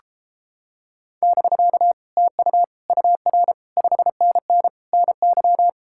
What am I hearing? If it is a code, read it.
Morse code, heard as a keyed tone, about 29 wpm, 700 Hz. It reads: <BK> TU UR 5NN NY